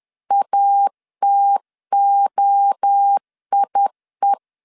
Morse code, heard as a keyed tone, about 10 wpm, 800 Hz. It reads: ATOIE
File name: CW / DET / 1 3 1500C S C 10wpm 3335